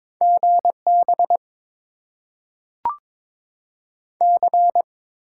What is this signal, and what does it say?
Morse code, heard as a keyed tone, about 22 wpm, 700 Hz. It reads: G B C
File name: Top 90 QSO Elements - 6x Character Spacing - 22wpm